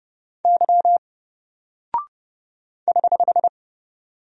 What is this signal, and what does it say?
Morse code, heard as a keyed tone, about 29 wpm, 700 Hz. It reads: Y <HH>
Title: Single Character (Letter:Number:Punctuation:Prosign) 30wpm